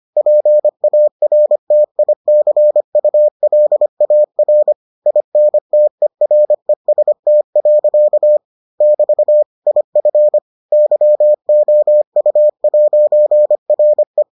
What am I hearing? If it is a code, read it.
Morse code, heard as a keyed tone, about 25 wpm, 600 Hz. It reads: PARTICULAR INTEREST. <BT> IF YOU'RE